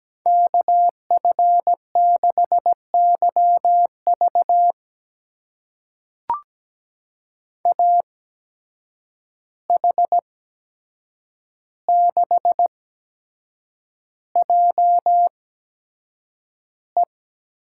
Morse code, heard as a keyed tone, about 17 wpm, spaced as for 7 wpm, 700 Hz. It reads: KF6YV AH6JE
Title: USA Call Signs - 5 Characters - 8x Character Spacing - 17wpm